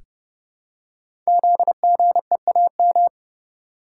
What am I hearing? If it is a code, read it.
Morse code, heard as a keyed tone, about 30 wpm, 700 Hz. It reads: ZGEAM
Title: Code Groups - 5 Letters - 30wpm